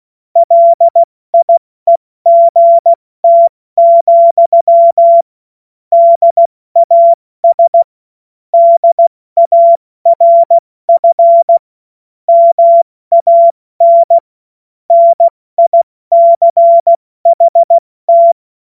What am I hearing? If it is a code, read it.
Morse code, heard as a keyed tone, about 16 wpm, 680 Hz. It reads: LIEGT, DAS DARF MAN NICHT
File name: Morsekurs / Witze / Witz 17. August 2021, 70 F80